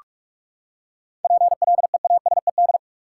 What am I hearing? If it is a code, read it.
Morse code, heard as a keyed tone, about 46 wpm, 700 Hz. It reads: PLEASED
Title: Seven Letter Words - 45wpm